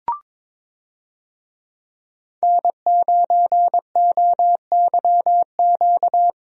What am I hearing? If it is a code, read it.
Morse code, heard as a keyed tone, about 22 wpm, 700 Hz. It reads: N9OYQ
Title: USA Call Signs - 5 Characters - Repeated 3x - 22wpm